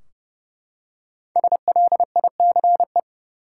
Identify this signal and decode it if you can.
Morse code, heard as a keyed tone, about 30 wpm, 700 Hz. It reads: SLICE
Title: Five Letter Words - 30wpm